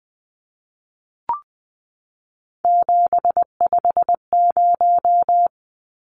Morse code, heard as a keyed tone, about 20 wpm, 700 Hz. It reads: 750
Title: Numbers - 3-Digit 20wpm